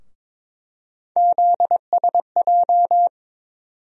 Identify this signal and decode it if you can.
Morse code, heard as a keyed tone, about 22 wpm, 700 Hz. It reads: ZSJ
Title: USA Call Sign Postfixes - 3 Letters 22wpm